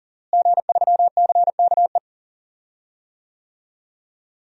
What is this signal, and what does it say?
Morse code, heard as a keyed tone, about 40 wpm, 700 Hz. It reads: G3CKE
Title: DXCC-100 Call Signs - 5 Characters - Repeated 2x - 40wpm